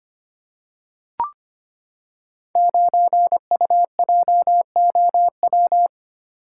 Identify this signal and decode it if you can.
Morse code, heard as a keyed tone, about 25 wpm, 690 Hz. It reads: 9UJOW
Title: Code Groups - 5 Letter:Numbers - 25wpm